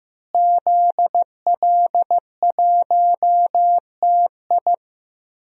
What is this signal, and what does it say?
Morse code, heard as a keyed tone, about 16 wpm, 700 Hz. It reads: ZL1TI
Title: DXCC-100 Call Signs - 5 Characters - Repeated 2x - 15wpm